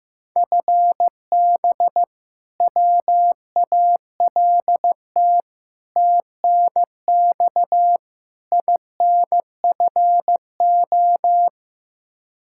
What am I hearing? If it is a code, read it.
Morse code, heard as a keyed tone, about 16 wpm, 700 Hz. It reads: FB WALT TNX INFO
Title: Practice QSO - Exchange 3 - More Info - 15wpm